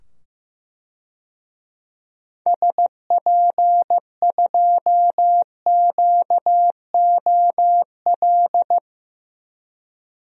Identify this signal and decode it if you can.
Morse code, heard as a keyed tone, about 15 wpm, 700 Hz. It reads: SP2QOL
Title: DXCC-100 Call Signs - 6 Characters - 15wpm